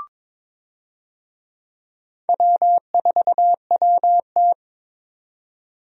Morse code, heard as a keyed tone, about 22 wpm, 700 Hz. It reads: W4WT
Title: USA Call Signs - 4 Characters - 22wpm